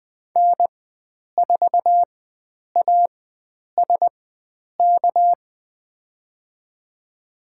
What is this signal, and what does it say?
Morse code, heard as a keyed tone, about 20 wpm, spaced as for 9 wpm, 700 Hz. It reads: N4ASK
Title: USA Call Signs - 5 Characters - 4x Character Spacing - 20wpm